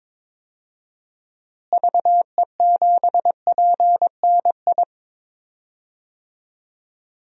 Morse code, heard as a keyed tone, about 21 wpm, 700 Hz. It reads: VE7PNI
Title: DXCC-100 Call Signs - 6 Characters - Repeated 3x - 22wpm